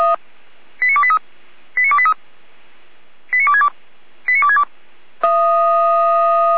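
Five-tone selective call (ZVEI), heard as quick runs of five tones, tones of about 70 ms. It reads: 89283 89283 89262 89262
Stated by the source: Five-tone ZVEI selective call received by SDR.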